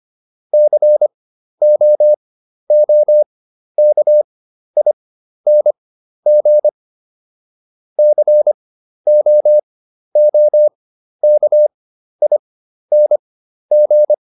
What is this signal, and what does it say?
Morse code, heard as a keyed tone, about 25 wpm, spaced as for 12 wpm, 600 Hz. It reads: COOKING COOKING